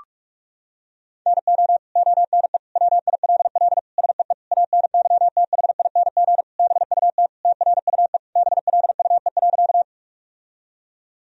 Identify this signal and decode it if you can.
Morse code, heard as a keyed tone, about 46 wpm, 700 Hz. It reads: NO ONE WILL SEE ANYTHING BUT TRUE BLUE.
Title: Sentences from Top 1000 Words - Review Part 3 - 45wpm